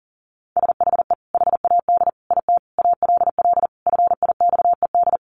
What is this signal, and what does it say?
Morse code, heard as a keyed tone, about 37 wpm, 700 Hz. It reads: SHE HAD IT ALL FIXED
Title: Sentences from Top 1100 Words 40wpm